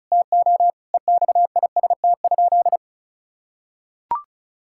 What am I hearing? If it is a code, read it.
Morse code, heard as a keyed tone, about 34 wpm, 700 Hz. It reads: TO EXIST?
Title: Sentences from Top 1000 Words - Review Part 5 - 35wpm